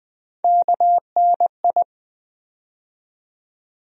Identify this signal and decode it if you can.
Morse code, heard as a keyed tone, about 20 wpm, 700 Hz. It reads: KNI
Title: USA Call Sign Postfixes - 3 Letters 20wpm